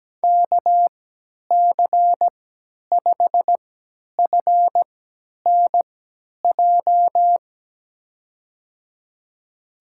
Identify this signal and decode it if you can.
Morse code, heard as a keyed tone, about 17 wpm, spaced as for 10 wpm, 700 Hz. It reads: KC5FNJ
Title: USA Call Signs - 6 Characters - 3x Character Spacing - 17wpm